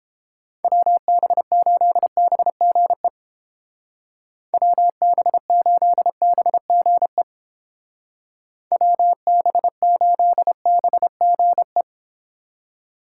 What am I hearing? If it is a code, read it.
Morse code, heard as a keyed tone, about 30 wpm, 700 Hz. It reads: WB8BGE WB8BGE WB8BGE